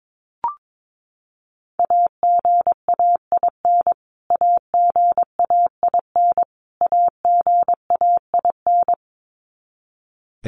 Morse code, heard as a keyed tone, about 22 wpm, 700 Hz. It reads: AGAIN AGAIN AGAIN
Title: Top 500 Words Repeated 22wpm